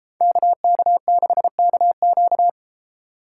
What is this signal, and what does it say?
Morse code, heard as a keyed tone, about 33 wpm, 700 Hz. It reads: KK6KQ